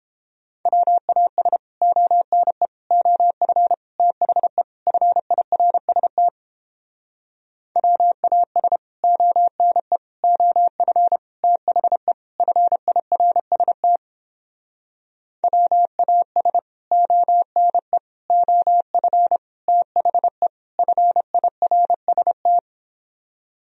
Morse code, heard as a keyed tone, about 29 wpm, 700 Hz. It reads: WAS ONE OF THE FIRST WAS ONE OF THE FIRST WAS ONE OF THE FIRST